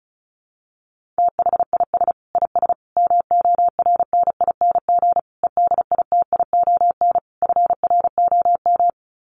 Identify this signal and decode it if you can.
Morse code, heard as a keyed tone, about 34 wpm, 700 Hz. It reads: THIS IS MORNING EDITION FROM
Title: Sets of 5 Words Encore2 35wpm